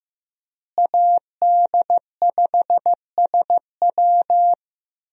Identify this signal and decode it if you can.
Morse code, heard as a keyed tone, about 15 wpm, 700 Hz. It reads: AD5SW
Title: USA Call Signs - 5 Characters - Repeated 3x - 15wpm